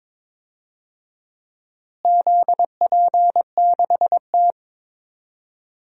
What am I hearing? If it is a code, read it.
Morse code, heard as a keyed tone, about 22 wpm, 700 Hz. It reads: ZP6T
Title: DXCC-100 Call Signs - 4 Characters - Repeated 3x - 22wpm